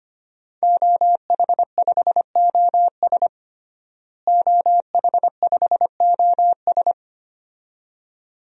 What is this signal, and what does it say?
Morse code, heard as a keyed tone, about 25 wpm, 700 Hz. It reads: OH5OS OH5OS
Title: DXCC-100 Call Signs - 5 Characters - Repeated 2x - 25wpm